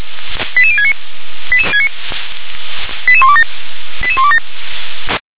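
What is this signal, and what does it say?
Five-tone selective call (ZVEI), heard as quick runs of five tones, tones of about 70 ms. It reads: 88069 88127 88127
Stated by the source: ZVEI five-tone selective call received on an SDR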